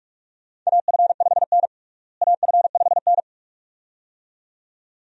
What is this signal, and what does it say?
Morse code, heard as a keyed tone, about 42 wpm, 700 Hz. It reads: AF5N AF5N
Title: USA Call Signs - 4 Characters - Repeated 2x - 45wpm